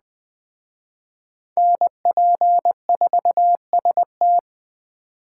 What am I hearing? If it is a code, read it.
Morse code, heard as a keyed tone, about 20 wpm, 700 Hz. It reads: NP4ST